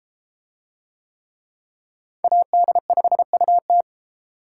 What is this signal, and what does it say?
Morse code, heard as a keyed tone, about 30 wpm, 700 Hz. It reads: AD5UT